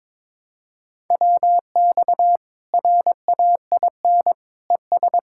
Morse code, heard as a keyed tone, about 22 wpm, 700 Hz. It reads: WX RAIN ES
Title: Practice QSO - Exchange 2 - Info - 22wpm